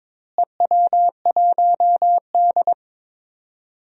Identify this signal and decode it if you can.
Morse code, heard as a keyed tone, about 22 wpm, 700 Hz. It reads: EW1D
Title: DXCC-100 Call Signs - 4 Characters - Repeated 3x - 22wpm